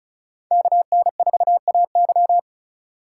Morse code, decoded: KN4AY

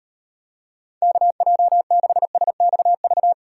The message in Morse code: KJ6SXV

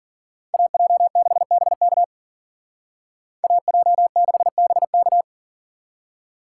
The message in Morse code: AJ6BK AJ6BK

